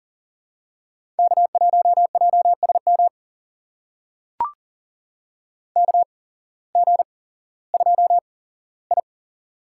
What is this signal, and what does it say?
Morse code, heard as a keyed tone, about 43 wpm, spaced as for 19 wpm, 700 Hz. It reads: K1JSM KG2I
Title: USA Call Signs - 8x Character Spacing - 40wpm